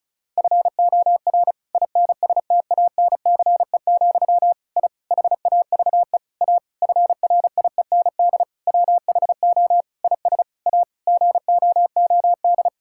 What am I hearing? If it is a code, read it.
Morse code, heard as a keyed tone, about 36 wpm, 700 Hz. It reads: FOR INSTANCE, I HAVE A FRIEND WHO IS A GOOD
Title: Sentences from Top 1000 Words - Review Part 5 - 35wpm